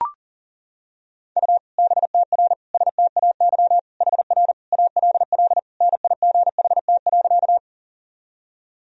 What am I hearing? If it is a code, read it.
Morse code, heard as a keyed tone, about 40 wpm, 700 Hz. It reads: U BTR STAY HR ALL NIGHT.
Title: Sentences from Top 500 Words and Abbreviations 40wpm